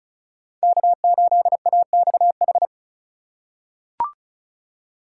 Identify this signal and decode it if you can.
Morse code, heard as a keyed tone, about 36 wpm, 700 Hz. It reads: K8AXH
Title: USA Call Signs - 5 Characters - 2x Character Spacing - 35wpm